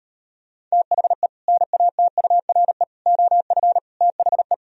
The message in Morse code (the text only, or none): THE NATURE OF THE